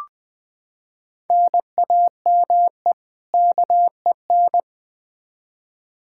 Morse code, heard as a keyed tone, about 20 wpm, 700 Hz. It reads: NAME KEN
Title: Practice QSO - Exchange 1 - Beginning - 20wpm